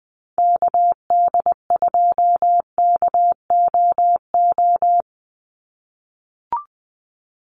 Morse code, heard as a keyed tone, about 20 wpm, 700 Hz. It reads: KD2KOO
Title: USA Call Signs 20wpm